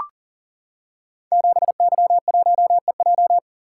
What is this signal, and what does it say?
Morse code, heard as a keyed tone, about 41 wpm, 690 Hz. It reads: 7Y1EJ
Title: Code Groups - 5 Letter:Numbers - 40wpm